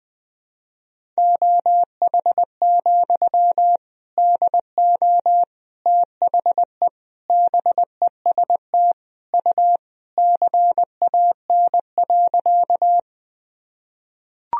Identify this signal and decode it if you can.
Morse code, heard as a keyed tone, about 20 wpm, 700 Hz. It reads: OH, DO THE BEST U CAN.
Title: Sentences from Top 500 Words and Abbreviations 20wpm